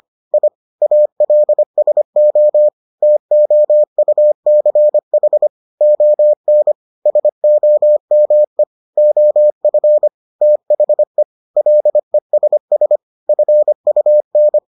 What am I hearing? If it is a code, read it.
Morse code, heard as a keyed tone, about 25 wpm, 600 Hz. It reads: I ALSO TOUCH ON SOME OF THE LESS FUN